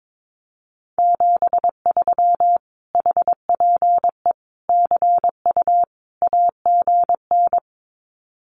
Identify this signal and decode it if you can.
Morse code, heard as a keyed tone, about 22 wpm, 700 Hz. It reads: 73 HPE CU AGN
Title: Common QSO Phrases Encore 22wpm